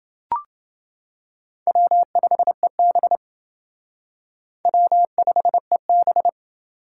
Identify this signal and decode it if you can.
Morse code, heard as a keyed tone, about 28 wpm, 700 Hz. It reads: W5EB W5EB